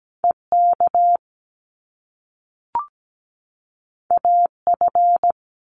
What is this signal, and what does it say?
Morse code, heard as a keyed tone, about 17 wpm, 700 Hz. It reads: EK AF
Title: Top Two Letter Combinations in Words 17wpm